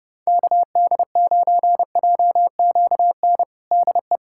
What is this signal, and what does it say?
Morse code, heard as a keyed tone, about 32 wpm, 700 Hz. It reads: KD9JQN DE
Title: K1USN SST QSO Practice 30wpm